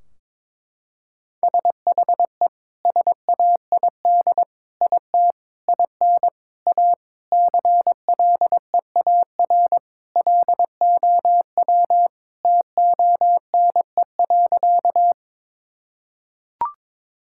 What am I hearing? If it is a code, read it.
Morse code, heard as a keyed tone, about 21 wpm, 700 Hz. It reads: SHE SAID IT IN A CLEAR LOW TONE.
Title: Sentences from Top 1600 Words 22wpm